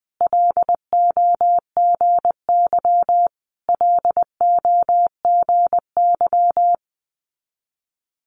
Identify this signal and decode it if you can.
Morse code, heard as a keyed tone, about 20 wpm, 700 Hz. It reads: LOGY LOGY
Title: Multi-letter Phonemes Repeated 3x 20wpm